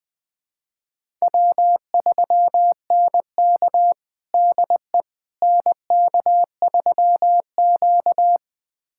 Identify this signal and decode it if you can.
Morse code, heard as a keyed tone, about 20 wpm, 700 Hz. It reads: W3NK DE NK3Q
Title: USA Callsign DE Callsign - 4 Characters - 20wpm